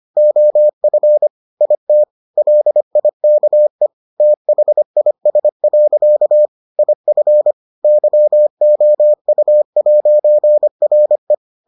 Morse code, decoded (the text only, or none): OF IT LIKE THIS. IF YOU'RE